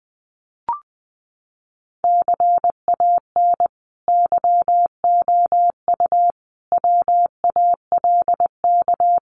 Morse code, heard as a keyed tone, about 20 wpm, 700 Hz. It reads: CAN YOU WALK
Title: Sentences from Top 1000 Words 20wpm